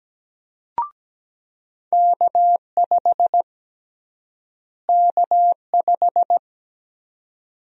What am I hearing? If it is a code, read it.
Morse code, heard as a keyed tone, about 17 wpm, 700 Hz. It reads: K5 K5
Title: USA Call Sign Prefixes - Repeated 2x - 17wpm